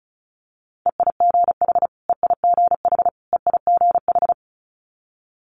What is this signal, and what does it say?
Morse code, heard as a keyed tone, about 36 wpm, 700 Hz. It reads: EIGH EIGH EIGH
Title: Multi-letter Phonemes Repeated 3x 35wpm